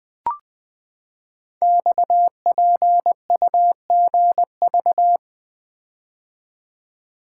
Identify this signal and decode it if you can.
Morse code, heard as a keyed tone, about 20 wpm, 700 Hz. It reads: XPUGV
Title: Code Groups - 5 Letters - 20wpm